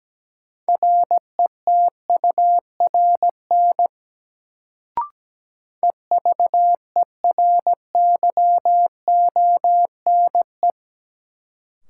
Morse code, heard as a keyed tone, about 17 wpm, 700 Hz. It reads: RETURN EVERYONE
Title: Sentences from Top 800 Words - New Word Drill - 17wpm